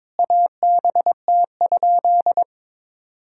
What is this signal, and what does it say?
Morse code, heard as a keyed tone, about 22 wpm, 700 Hz. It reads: ABT?
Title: Sentences from Top 100 Words and Abbreviations 22wpm